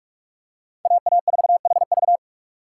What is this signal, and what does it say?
Morse code, heard as a keyed tone, about 42 wpm, 700 Hz. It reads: AA4HV
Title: K1USN SST QSO Practice - 45wpm